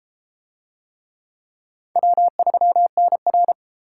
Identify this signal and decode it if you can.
Morse code, heard as a keyed tone, about 33 wpm, 700 Hz. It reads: W3NR